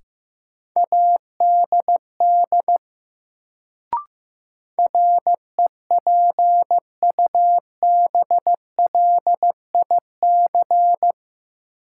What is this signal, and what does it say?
Morse code, heard as a keyed tone, about 15 wpm, 700 Hz. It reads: ADD REPUBLIC
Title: Sentences from Top 1500 Words - New Word Drill Alt - 15wpm